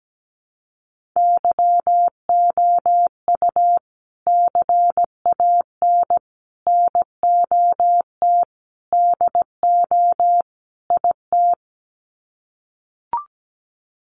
Morse code, heard as a keyed tone, about 17 wpm, 700 Hz. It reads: YOU CAN NOT DO IT
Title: Sets of 5 Words Encore2 17wpm